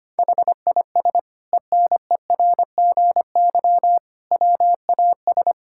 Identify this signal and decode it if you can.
Morse code, heard as a keyed tone, about 25 wpm, 700 Hz. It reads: HIS ENERGY WAS
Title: Sentences from Top 1000 Words - Review Part 6 - 25wpm